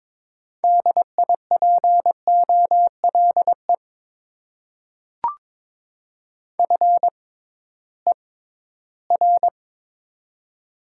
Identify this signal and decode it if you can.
Morse code, heard as a keyed tone, about 22 wpm, spaced as for 14 wpm, 700 Hz. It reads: DIPOLE FER
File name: Top 90 QSO Elements - 6x Character Spacing - 22wpm